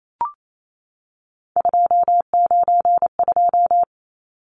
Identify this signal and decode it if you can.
Morse code, heard as a keyed tone, about 28 wpm, 700 Hz. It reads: 292
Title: Numbers - 3-Digit 28wpm